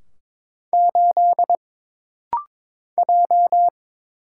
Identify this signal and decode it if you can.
Morse code, heard as a keyed tone, about 22 wpm, 700 Hz. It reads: 8 J
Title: Single Letter-Number - Rapid-Fire 22wpm